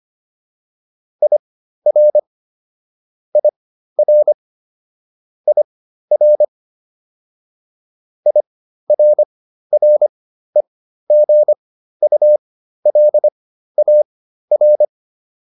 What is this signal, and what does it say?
Morse code, heard as a keyed tone, about 25 wpm, spaced as for 13 wpm, 600 Hz. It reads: IR IR IR IRREGULAR